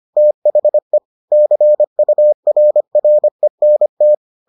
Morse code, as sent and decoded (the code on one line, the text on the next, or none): - .... / -.-. ..- .-. .-. -. -
THE CURRENT